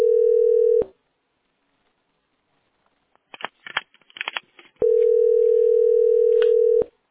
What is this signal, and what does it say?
Ringback tone (North America), 440 + 480 Hz.